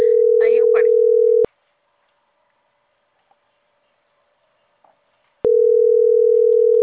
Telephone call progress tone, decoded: ringback tone

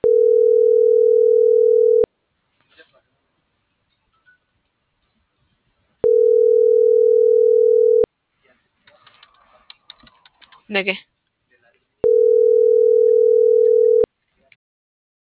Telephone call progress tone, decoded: ringback tone

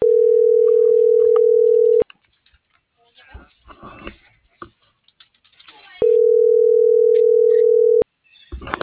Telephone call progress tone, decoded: ringback tone

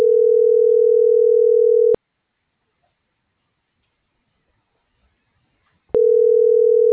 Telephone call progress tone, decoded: ringback tone

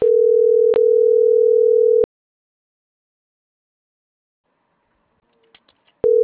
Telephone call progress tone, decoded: ringback tone

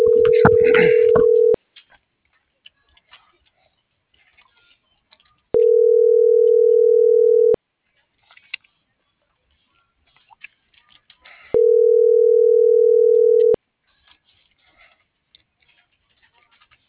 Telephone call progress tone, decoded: ringback tone

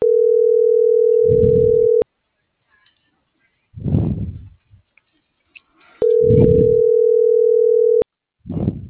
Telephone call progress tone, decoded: ringback tone

